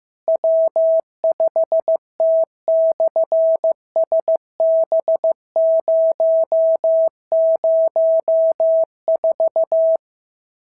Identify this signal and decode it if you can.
Morse code, heard as a keyed tone, about 15 wpm, 650 Hz. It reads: W5T/SB004
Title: SOTA QSO Practice 15wpm